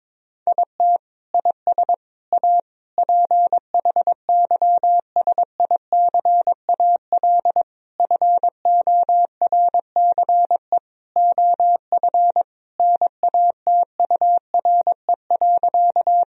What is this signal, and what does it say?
Morse code, heard as a keyed tone, about 22 wpm, 700 Hz. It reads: IT IS A PHYSICAL FORCE OF NATURE.
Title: Sentences from Top 1000 Words - Review Part 4 - 22wpm